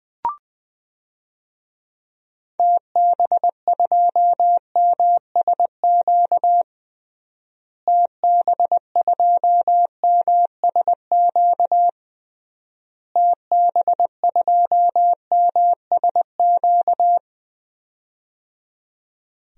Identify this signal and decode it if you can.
Morse code, heard as a keyed tone, about 20 wpm, 700 Hz. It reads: TB2MSQ TB2MSQ TB2MSQ
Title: DXCC-100 Call Signs - 6 Characters - Repeated 3x - 20wpm